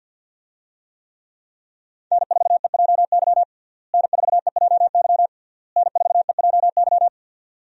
Morse code, heard as a keyed tone, about 48 wpm, 700 Hz. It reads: N4EJY N4EJY N4EJY
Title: USA Call Signs - 5 Characters - Repeated 3x - 50wpm